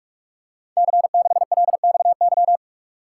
Morse code, heard as a keyed tone, about 44 wpm, 700 Hz. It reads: C6LXY
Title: Code Groups - 5 Letter:Numbers - 45wpm